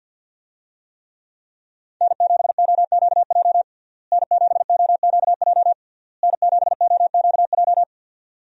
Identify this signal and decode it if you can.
Morse code, heard as a keyed tone, about 50 wpm, 700 Hz. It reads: N7OQJ N7OQJ N7OQJ
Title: USA Call Signs - 5 Characters - Repeated 3x - 50wpm